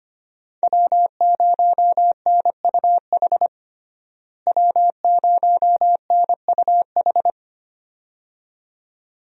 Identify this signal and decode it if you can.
Morse code, heard as a keyed tone, about 25 wpm, 700 Hz. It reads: W0NUH W0NUH
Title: USA Call Signs - 5 Characters - Repeated 2x - 25wpm